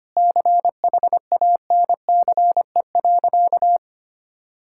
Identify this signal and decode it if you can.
Morse code, heard as a keyed tone, about 25 wpm, 700 Hz. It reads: CHANCE.
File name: practice / 25wpm / Sentences with 3-Word Phrases - 25wpm